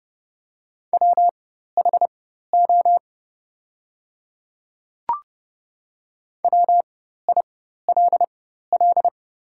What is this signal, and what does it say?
Morse code, heard as a keyed tone, about 30 wpm, spaced as for 14 wpm, 700 Hz. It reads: WHO WILL
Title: 4x Character Spacing - Top 100 Words - 30wpm